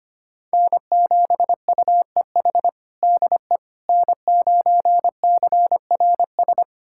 Morse code, heard as a keyed tone, about 25 wpm, 700 Hz. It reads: N7UEH DE N9CRS
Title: USA Callsign DE Callsign - 5 Characters - 25wpm